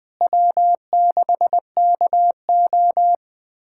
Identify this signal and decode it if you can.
Morse code, heard as a keyed tone, about 20 wpm, 700 Hz. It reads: W6KO
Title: USA Callsign DE Callsign - 4 Characters - 20wpm